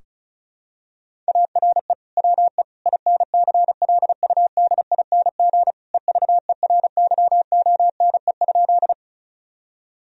Morse code, heard as a keyed tone, about 36 wpm, 700 Hz. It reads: ARE WE INCLUDING EVERYONE?
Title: Sentences from Top 1000 Words - Review Part 4 - 35wpm